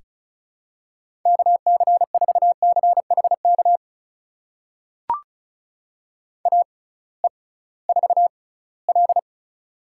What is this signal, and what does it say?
Morse code, heard as a keyed tone, about 36 wpm, spaced as for 21 wpm, 700 Hz. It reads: KC4CHK AE4L
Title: USA Call Signs - 6x Character Spacing - 35wpm